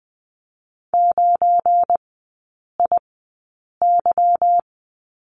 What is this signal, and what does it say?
Morse code, heard as a keyed tone, about 20 wpm, spaced as for 8 wpm, 700 Hz. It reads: 9IY
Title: 178 Lesson 29e - Slash - Three Characters 20wpm